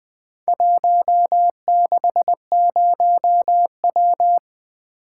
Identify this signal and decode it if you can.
Morse code, heard as a keyed tone, about 20 wpm, 700 Hz. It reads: 160W
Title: Four Letter-Number Combinations - 20wpm